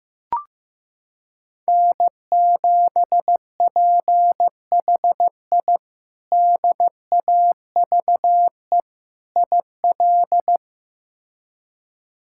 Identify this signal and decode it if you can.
Morse code, heard as a keyed tone, about 15 wpm, 700 Hz. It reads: N7PHI DAVE IL
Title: K1USN SST QSO Practice - 15wpm